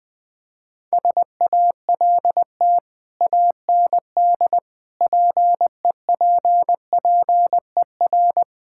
Morse code, heard as a keyed tone, about 20 wpm, 700 Hz. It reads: SALT AND PEPPER
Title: Build A Phrase - 5-Word Phrases - 20wpm